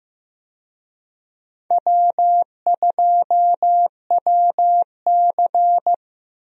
Morse code, heard as a keyed tone, about 15 wpm, 700 Hz. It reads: W2WC
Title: USA Call Signs - 4 Characters - 15wpm